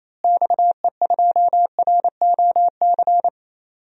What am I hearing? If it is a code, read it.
Morse code, heard as a keyed tone, about 28 wpm, 700 Hz. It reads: XE2ROC
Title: DXCC-100 Call Signs - 6 Characters - Repeated 2x - 28wpm